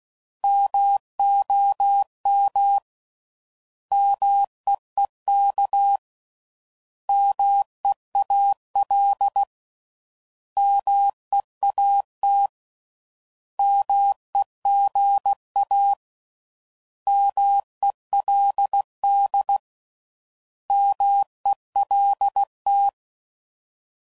Morse code, heard as a keyed tone, about 16 wpm, 800 Hz. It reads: MOM MEEK MEAL MEAT MEGA MELD MELT